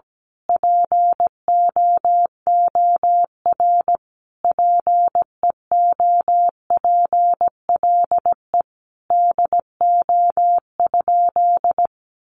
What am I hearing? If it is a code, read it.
Morse code, heard as a keyed tone, about 17 wpm, 700 Hz. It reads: POOR PEOPLE DO?
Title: Sentences from Top 1100 Words 17wpm